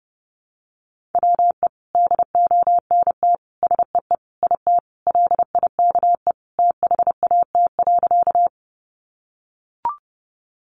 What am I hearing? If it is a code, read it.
Morse code, heard as a keyed tone, about 30 wpm, 700 Hz. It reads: WE DONT SEE IT LIKE THAT.
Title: Sentences from Top 300 Words 30wpm